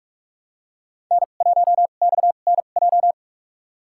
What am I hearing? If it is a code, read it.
Morse code, heard as a keyed tone, about 46 wpm, spaced as for 32 wpm, 700 Hz. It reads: N1XNJ
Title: USA Call Signs - 5 Characters - 2x Character Spacing - 45wpm